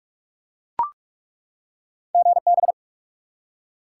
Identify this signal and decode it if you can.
Morse code, heard as a keyed tone, about 48 wpm, 700 Hz. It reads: GB
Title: Top 90 QSO Elements 45wpm